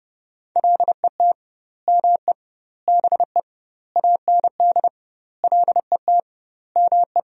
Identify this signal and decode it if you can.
Morse code, heard as a keyed tone, about 32 wpm, 700 Hz. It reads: LET ME BE AND LET ME
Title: 2x Word-Spacing - Sentences from Top 300 Words - 30wpm